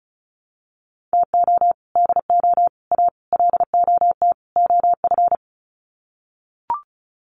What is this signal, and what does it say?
Morse code, heard as a keyed tone, about 34 wpm, 700 Hz. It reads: TO DO A LOT OF
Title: Sets of 5 Words Encore 35wpm